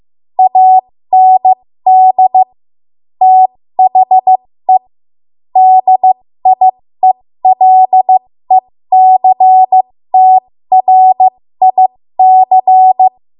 Morse code, heard as a keyed tone, about 15 wpm, spaced as for 13 wpm, 750 Hz. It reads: AND THE DIELECTRIC
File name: Morse / 13WPM